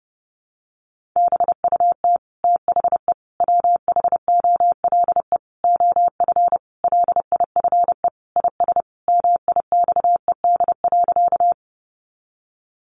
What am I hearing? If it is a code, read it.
Morse code, heard as a keyed tone, about 30 wpm, 700 Hz. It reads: BUT THE WHOLE OF LIFE IS MIXED.
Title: Sentences from Top 1500 Words 30wpm